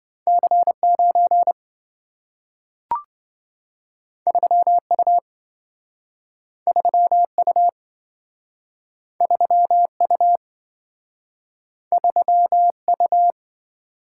Morse code, heard as a keyed tone, about 27 wpm, 700 Hz. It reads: C9 3U 3U 3U 3U